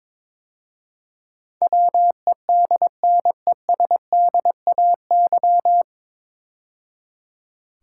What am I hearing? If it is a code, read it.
Morse code, heard as a keyed tone, about 22 wpm, 700 Hz. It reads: WEDNESDAY
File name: 9 Letter Words - First 3 Letters Given - 22wpm